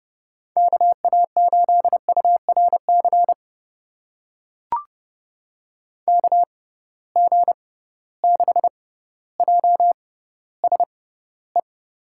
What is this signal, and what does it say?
Morse code, heard as a keyed tone, about 31 wpm, spaced as for 15 wpm, 700 Hz. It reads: KA8URC KG6JSE
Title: USA Call Signs - 6 Characters - 6x Character Spacing - 30wpm